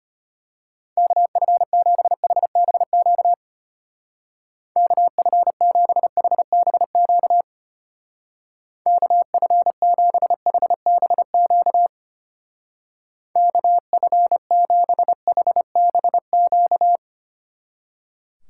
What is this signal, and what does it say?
Morse code, heard as a keyed tone, about 31 wpm, 700 Hz. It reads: KF7HBQ KF7HBQ KF7HBQ KF7HBQ
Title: Speed Racing - USA 2x3 Call Signs 25wpm